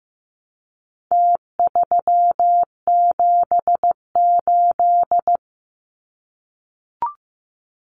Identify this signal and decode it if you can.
Morse code, heard as a keyed tone, about 15 wpm, 700 Hz. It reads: T378
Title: Numbers - Contest Style - 4-Digit 15wpm